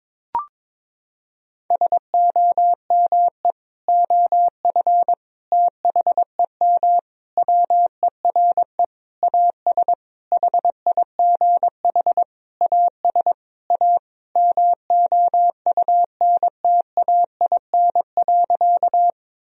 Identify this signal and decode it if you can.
Morse code, heard as a keyed tone, about 22 wpm, 700 Hz. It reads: SOME OF THEM WERE AS HIGH AS A MOUNTAIN.
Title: Sentences from Top 1000 Words - Review Part 1 - 22wpm